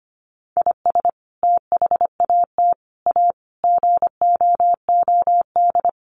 Morse code, heard as a keyed tone, about 25 wpm, 700 Hz. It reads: IS THAT A GOOD
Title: Sentences from Top 1500 Words 25wpm